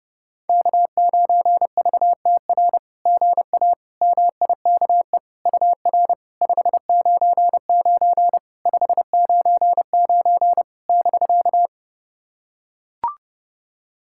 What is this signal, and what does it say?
Morse code, heard as a keyed tone, about 30 wpm, 700 Hz. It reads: K9VTR GA MIKE UR 599 599 <BK>
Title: SOTA QSO Practice 30wpm